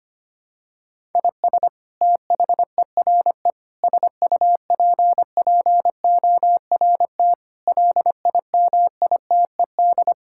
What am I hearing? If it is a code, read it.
Morse code, heard as a keyed tone, about 25 wpm, 700 Hz. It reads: IS THERE SUPPORT LIMITED
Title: Sentences from Top 1000 Words - Review Part 6 - 25wpm